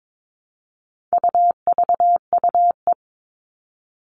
Morse code, heard as a keyed tone, about 22 wpm, 700 Hz. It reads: UVUE
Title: All Cut Numbers - 4-Digit 22wpm